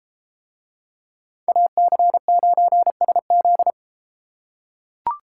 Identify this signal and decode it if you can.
Morse code, heard as a keyed tone, about 31 wpm, 700 Hz. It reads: AC9SZ